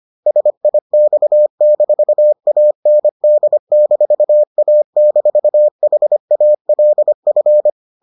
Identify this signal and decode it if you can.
Morse code, heard as a keyed tone, about 25 wpm, 600 Hz. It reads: SIX-AND-A-HALF